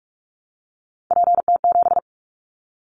Morse code, heard as a keyed tone, about 46 wpm, 700 Hz. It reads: PT7
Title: Callsign Prefixes for Top 50 DXCC - 45wpm